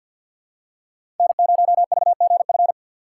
Morse code, heard as a keyed tone, about 53 wpm, 700 Hz. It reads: N0VGF
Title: USA Call Signs - 5 Characters - 4x Character Spacing - 50wpm